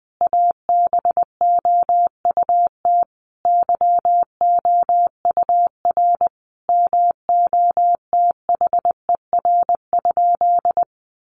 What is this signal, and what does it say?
Morse code, heard as a keyed tone, about 20 wpm, 700 Hz. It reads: ABOUT YOUR MOTHER?
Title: Sentences from Top 200 Words 20wpm